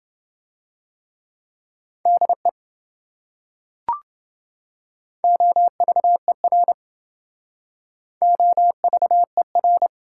Morse code, heard as a keyed tone, about 27 wpm, 700 Hz. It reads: DE OVER OVER